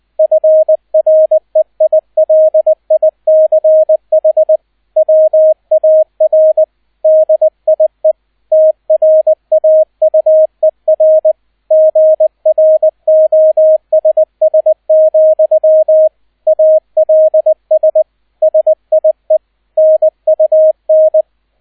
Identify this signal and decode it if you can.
Morse code, heard as a keyed tone, about 20 wpm, 630 Hz, with radio noise. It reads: FREILICH WAR DIE TRAUER GROSS, ALS SIE NUN